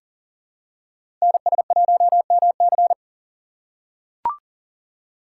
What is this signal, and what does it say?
Morse code, heard as a keyed tone, about 40 wpm, 700 Hz. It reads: NS1MC
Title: USA Call Signs - 5 Characters - 2x Character Spacing - 40wpm